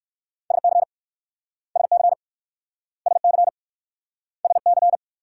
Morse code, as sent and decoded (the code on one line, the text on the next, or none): ... -.-. / ... -.-. / ... -.-. / ... -.-.
SC SC SC SC